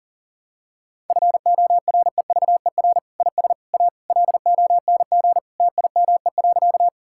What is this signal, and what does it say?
Morse code, heard as a keyed tone, about 40 wpm, 700 Hz. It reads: FOREVER IS A LONG TIME.
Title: Sentences from Top 1600 Words 40wpm